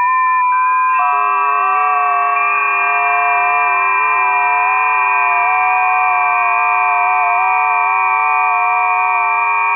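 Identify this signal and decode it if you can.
FT8.
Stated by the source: FT8 signals recorded on a busy 20 m band.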